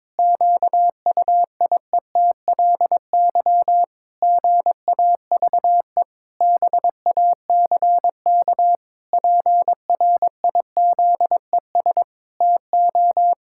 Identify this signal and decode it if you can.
Morse code, heard as a keyed tone, about 22 wpm, 700 Hz. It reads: QUIETLY GAVE BACK PRIZES TO
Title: Pangram Challenge v2 - 22wpm